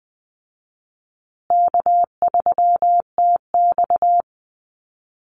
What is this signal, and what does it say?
Morse code, heard as a keyed tone, about 20 wpm, 700 Hz. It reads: K3TX
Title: Easy Call Signs Repeated 3x - 20wpm